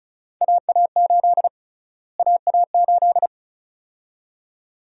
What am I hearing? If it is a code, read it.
Morse code, heard as a keyed tone, about 36 wpm, 700 Hz. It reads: AA8 AA8
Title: USA Call Sign Prefixes - Repeated 2x - 35wpm